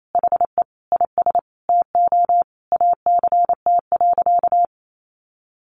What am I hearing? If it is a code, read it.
Morse code, heard as a keyed tone, about 28 wpm, 700 Hz. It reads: HE IS TO ACT.
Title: Sentences from Top 1200 Words 28wpm